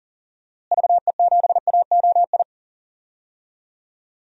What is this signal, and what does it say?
Morse code, heard as a keyed tone, about 40 wpm, 700 Hz. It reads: VE7AOI
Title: DXCC-100 Call Signs - 6 Characters - Repeated 3x - 40wpm